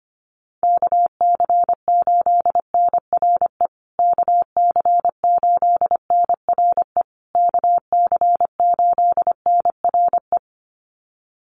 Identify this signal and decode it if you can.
Morse code, heard as a keyed tone, about 25 wpm, 700 Hz. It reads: KC8NRE KC8NRE KC8NRE
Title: USA Call Signs Repeated 3x 25wpm